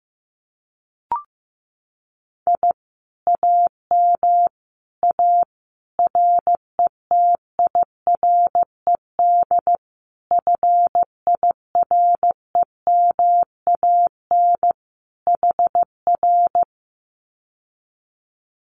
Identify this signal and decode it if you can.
Morse code, heard as a keyed tone, about 15 wpm, 700 Hz. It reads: I AM A RETIRED FIREMAN HR
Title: Common QSO Phrases Encore 15wpm